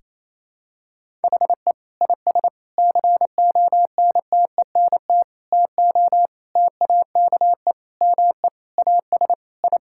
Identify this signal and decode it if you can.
Morse code, heard as a keyed tone, about 28 wpm, 700 Hz. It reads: HE IS CONTENT TO TAKE ME AS I
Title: Sentences from Top 1000 Words - Review Part 6 - 28wpm